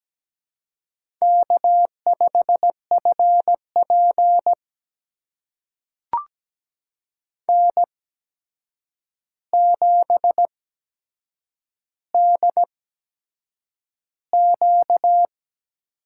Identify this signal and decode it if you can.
Morse code, heard as a keyed tone, about 17 wpm, spaced as for 7 wpm, 700 Hz. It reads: K5FP N7DQ
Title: USA Call Signs - 8x Character Spacing - 17wpm